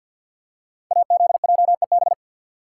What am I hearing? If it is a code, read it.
Morse code, heard as a keyed tone, about 49 wpm, 700 Hz. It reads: AZJEB
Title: Code Groups - 5 Letter:Numbers - 50wpm